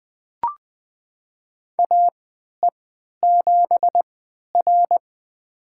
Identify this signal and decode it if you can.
Morse code, heard as a keyed tone, about 20 wpm, spaced as for 11 wpm, 700 Hz. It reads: AE7R
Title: USA Call Signs - 3x Character Spacing - 20wpm